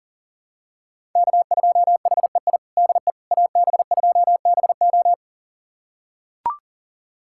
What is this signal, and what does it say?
Morse code, heard as a keyed tone, about 40 wpm, 700 Hz. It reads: K2HEI DE AB2BO
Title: USA Callsign DE Callsign - 5 Characters - 40wpm